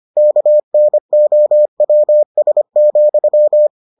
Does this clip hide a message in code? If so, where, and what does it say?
Yes: Morse code from 0.2 to 3.7 s, heard as a keyed tone, about 25 wpm, 600 Hz. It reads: KNOWS,